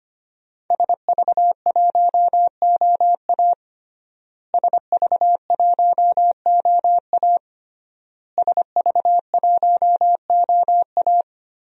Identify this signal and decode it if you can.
Morse code, heard as a keyed tone, about 25 wpm, 700 Hz. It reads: SV1OA SV1OA SV1OA
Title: DXCC-100 Call Signs - 5 Characters - Repeated 3x - 25wpm